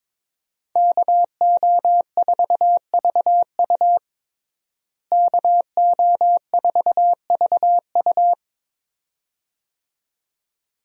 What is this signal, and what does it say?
Morse code, heard as a keyed tone, about 22 wpm, 700 Hz. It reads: KO4VU KO4VU
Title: USA Call Signs - 5 Characters - Repeated 2x - 22wpm